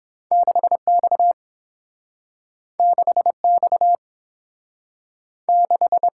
Morse code, heard as a keyed tone, about 30 wpm, 700 Hz. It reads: 6X 6X 6